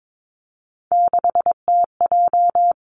Morse code, heard as a keyed tone, about 22 wpm, 700 Hz. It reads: 6TJ
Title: Three Challenging Letter-Number Combinations 22wpm